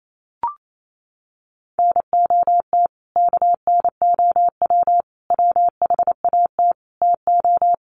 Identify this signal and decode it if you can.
Morse code, heard as a keyed tone, about 28 wpm, 700 Hz. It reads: NOT KNOW WHAT TO
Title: Sets of 5 Words 28wpm